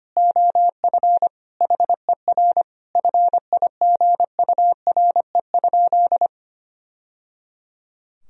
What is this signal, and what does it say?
Morse code, heard as a keyed tone, about 25 wpm, 700 Hz. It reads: OF HER FIGURE?
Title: Sentences from Top 1000 Words - Review Part 2 - 25wpm